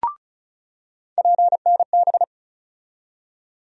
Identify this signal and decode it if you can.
Morse code, heard as a keyed tone, about 36 wpm, 700 Hz. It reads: PNB
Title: USA Call Sign Postfixes - 3 Letters 35wpm